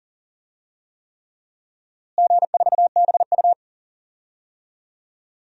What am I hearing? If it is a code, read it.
Morse code, heard as a keyed tone, about 40 wpm, 700 Hz. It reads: G4BU
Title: DXCC-100 Call Signs - 4 Characters - Repeated 3x - 40wpm